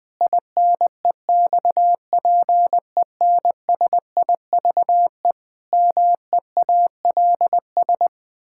Morse code, heard as a keyed tone, about 20 wpm, 700 Hz. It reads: INEXPENSIVE MEALS
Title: Pangram Challenge v2 - 20wpm